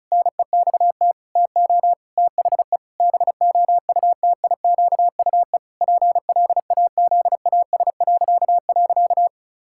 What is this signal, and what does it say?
Morse code, heard as a keyed tone, about 34 wpm, 700 Hz. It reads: NEXT TO THE BOUTIQUE PLAZAS..
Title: Pangram Challenge v2 - 35wpm